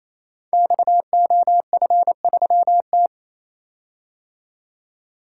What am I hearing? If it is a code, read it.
Morse code, heard as a keyed tone, about 28 wpm, 700 Hz. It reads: XOF3T
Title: Code Groups - 5 Letter:Numbers - 28wpm